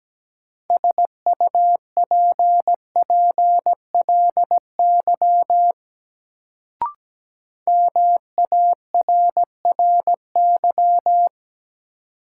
Morse code, heard as a keyed tone, about 17 wpm, 700 Hz. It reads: SUPPLY MARRY
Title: Sentences from Top 1200 Words - New Word Drill Alt - 17wpm